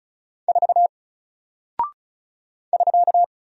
Morse code, heard as a keyed tone, about 36 wpm, 700 Hz. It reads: 4 <SK>
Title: Single Character (Letter:Number:Punctuation:Prosign) 35wpm